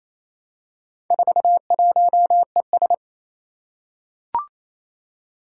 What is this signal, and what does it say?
Morse code, heard as a keyed tone, about 29 wpm, 700 Hz. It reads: 41ES
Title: Four Challenging Letter-Number Combinations 28wpm